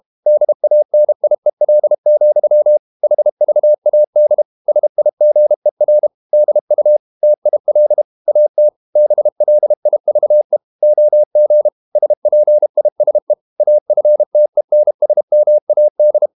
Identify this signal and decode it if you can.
Morse code, heard as a keyed tone, about 32 wpm, 600 Hz. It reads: DANIEL, HVAD SIGER DU TIL AT BLIVE OG SPISE AFTENSMAD
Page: Ferie 32 wpm